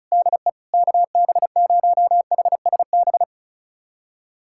Morse code, heard as a keyed tone, about 36 wpm, 700 Hz. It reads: DE KB0HSB
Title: USA Callsign DE Callsign - 6 Characters - 35wpm